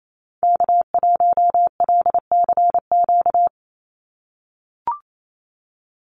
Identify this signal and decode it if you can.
Morse code, heard as a keyed tone, about 28 wpm, 700 Hz. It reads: K1LCQ
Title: USA Call Signs 28wpm